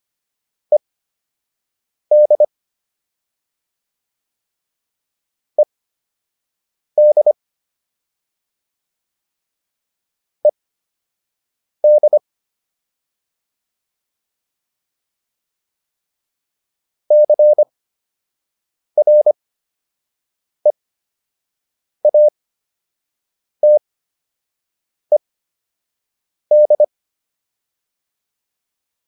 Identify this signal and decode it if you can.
Morse code, heard as a keyed tone, about 25 wpm, spaced as for 6 wpm, 600 Hz. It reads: ED ED ED CREATED